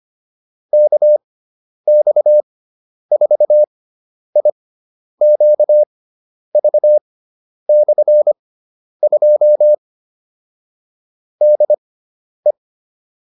Morse code, heard as a keyed tone, about 25 wpm, spaced as for 10 wpm, 600 Hz. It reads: KX4IQV/2 DE